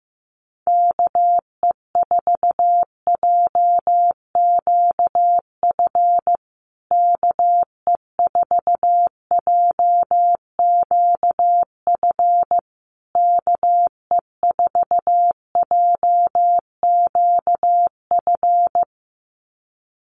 Morse code, heard as a keyed tone, about 15 wpm, 700 Hz. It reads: KE4JQF KE4JQF KE4JQF
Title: USA Call Signs Repeated 3x 15wpm